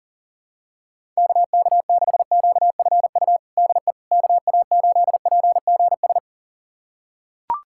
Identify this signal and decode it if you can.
Morse code, heard as a keyed tone, about 40 wpm, 700 Hz. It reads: KK6QFU DE KA8PGS
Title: USA Callsign DE Callsign - 6 Characters - 40wpm